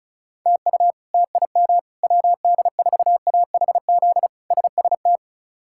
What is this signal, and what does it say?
Morse code, heard as a keyed tone, about 34 wpm, 700 Hz. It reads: TU TIM WD4AHZ SST
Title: K1USN SST QSO Practice - 35wpm